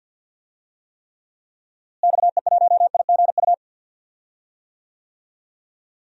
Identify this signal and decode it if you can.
Morse code, heard as a keyed tone, about 49 wpm, 700 Hz. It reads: XE1IGU
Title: DXCC-100 Call Signs - 6 Characters - Repeated 3x - 50wpm